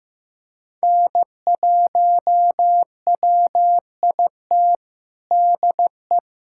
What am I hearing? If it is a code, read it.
Morse code, heard as a keyed tone, about 15 wpm, 700 Hz. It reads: N1WIT DE
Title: USA Callsign DE Callsign - 5 Characters - 15wpm